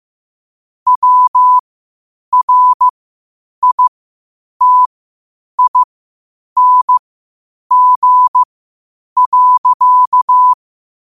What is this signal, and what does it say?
Morse code, heard as a keyed tone, about 15 wpm, spaced as for 8 wpm, 1000 Hz. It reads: WRITING.
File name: qotdfiles / 8wpm